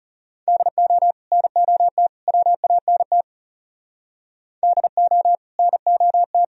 Morse code, heard as a keyed tone, about 35 wpm, 700 Hz. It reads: DO NOT WANT DO NOT
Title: Speed-Racing - Sets of 3 Words 30wpm